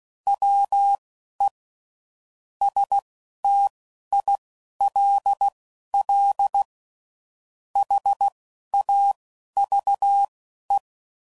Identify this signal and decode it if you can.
Morse code, heard as a keyed tone, about 16 wpm, spaced as for 11 wpm, 800 Hz. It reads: WE STILL HAVE
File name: CW Code Groups @10WPM / QSOs / QSO WA7QOX